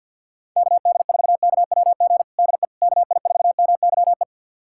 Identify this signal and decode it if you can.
Morse code, heard as a keyed tone, about 50 wpm, 700 Hz. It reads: KD4KWG DE KI4MYE
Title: USA Callsign DE Callsign 50wpm